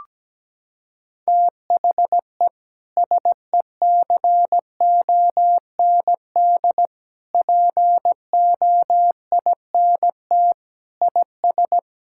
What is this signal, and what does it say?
Morse code, heard as a keyed tone, about 17 wpm, 700 Hz. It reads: THE SECOND POINT IS
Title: Sentences from Top 1000 Words - Review Part 5 - 17wpm